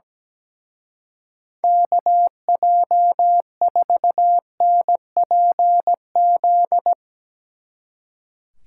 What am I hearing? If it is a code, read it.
Morse code, heard as a keyed tone, about 19 wpm, 700 Hz. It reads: KJ4NPZ
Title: Speed Racing - USA 2x3 Call Signs 17wpm